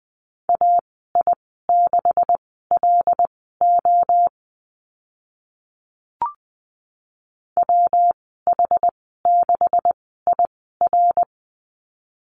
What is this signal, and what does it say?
Morse code, heard as a keyed tone, about 20 wpm, spaced as for 15 wpm, 700 Hz. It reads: AI6LO WH6IR
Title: 128 Lesson 21g - Number 6 - Call Signs 20wpm